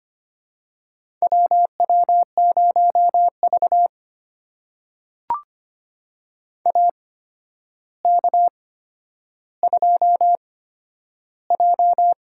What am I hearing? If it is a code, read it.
Morse code, heard as a keyed tone, about 25 wpm, spaced as for 11 wpm, 700 Hz. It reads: WW0V AK2J
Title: USA Call Signs - 4 Characters - 8x Character Spacing - 25wpm